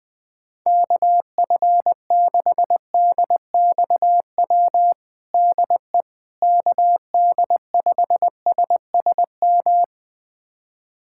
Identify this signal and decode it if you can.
Morse code, heard as a keyed tone, about 21 wpm, 700 Hz. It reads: KF6DXW DE KD5SSM
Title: USA Callsign DE Callsign - 6 Characters - 20wpm